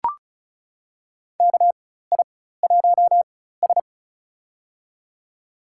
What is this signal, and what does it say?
Morse code, heard as a keyed tone, about 36 wpm, spaced as for 16 wpm, 700 Hz. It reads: KI1S
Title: USA Call Signs - 4 Characters - 4x Character Spacing - 35wpm